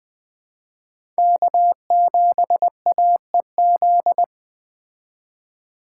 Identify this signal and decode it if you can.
Morse code, heard as a keyed tone, about 21 wpm, 700 Hz. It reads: K7AEZ